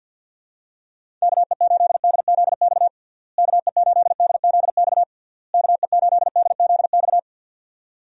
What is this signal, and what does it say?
Morse code, heard as a keyed tone, about 50 wpm, 700 Hz. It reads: KE8DZX KE8DZX KE8DZX
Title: USA Call Signs - 6 Characters - Repeated 3x - 50wpm